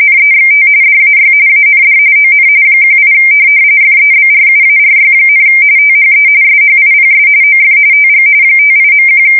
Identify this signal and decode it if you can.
RTTY, about 45.45 baud. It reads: NEXT DAY AND ON 18 FEBRUARY, THE SHIPS OF TG 58.1
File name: fsk170 / USS-Massachusetts